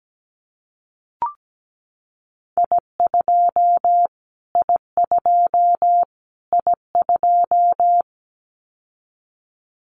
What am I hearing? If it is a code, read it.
Morse code, heard as a keyed tone, about 17 wpm, 700 Hz. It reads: I2 I2 I2
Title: Callsign Prefixes for Top 25 DXCC - Repeated 3x - 17wpm